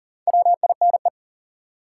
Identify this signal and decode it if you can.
Morse code, heard as a keyed tone, about 40 wpm, 700 Hz. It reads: WINE
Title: Four Letter Words - 40wpm